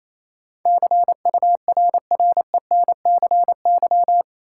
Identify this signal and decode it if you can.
Morse code, heard as a keyed tone, about 28 wpm, 700 Hz. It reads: CURRENCY
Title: Sentences from Top 1500 Words - New Word Drill - 28wpm